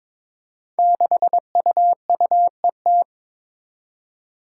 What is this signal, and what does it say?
Morse code, heard as a keyed tone, about 22 wpm, 700 Hz. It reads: 6UUET
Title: Five Challenging Letter-Number Combinations 22wpm